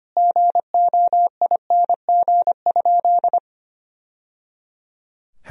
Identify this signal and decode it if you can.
Morse code, heard as a keyed tone, about 25 wpm, 700 Hz. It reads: GOING?
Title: Sentences from Top 1000 Words - Review Part 4 - 25wpm